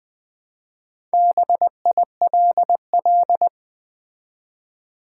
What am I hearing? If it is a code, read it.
Morse code, heard as a keyed tone, about 20 wpm, 700 Hz. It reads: BILL
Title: Four Letter Words - 20wpm